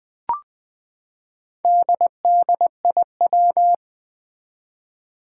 Morse code, heard as a keyed tone, about 20 wpm, 700 Hz. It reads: DDIW
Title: Four Challenging Letter-Number Combinations 20wpm